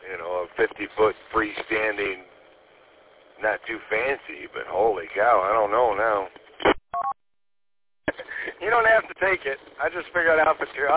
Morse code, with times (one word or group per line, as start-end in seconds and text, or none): none